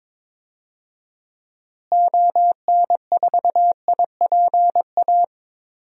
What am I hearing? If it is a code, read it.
Morse code, heard as a keyed tone, about 21 wpm, 700 Hz. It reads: ON4IPA
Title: DXCC-100 Call Signs - 6 Characters - Repeated 3x - 22wpm